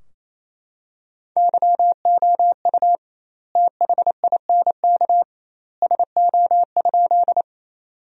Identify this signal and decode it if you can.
Morse code, heard as a keyed tone, about 28 wpm, 700 Hz. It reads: YOU THINK SO?
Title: 2x Word-Spacing - Sentences from Top 200 Words - 28wpm